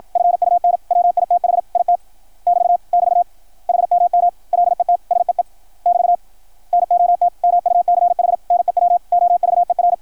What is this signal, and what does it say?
Morse code, heard as a keyed tone, about 53 wpm, 690 Hz, with radio noise. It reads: FUN WITH IT <BT> <BT> HMM LET SEE <BT> NOT MUCH NEW OVER
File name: archived / CRSnet